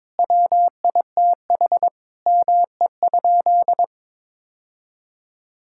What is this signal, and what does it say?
Morse code, heard as a keyed tone, about 22 wpm, 700 Hz. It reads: WITH ME?
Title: Sentences from Top 1000 Words - Review Part 5 - 22wpm